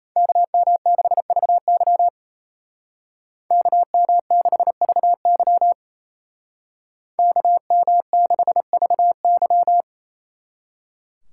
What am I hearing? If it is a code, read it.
Morse code, heard as a keyed tone, about 33 wpm, 700 Hz. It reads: KM6VY KM6VY KM6VY
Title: Speed Racing - USA Call Signs 28wpm